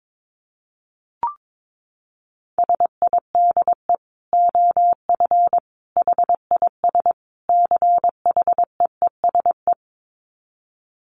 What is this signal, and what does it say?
Morse code, heard as a keyed tone, about 22 wpm, 700 Hz. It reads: SIDE OF HIS CHEESE
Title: Dit Challenge 22wpm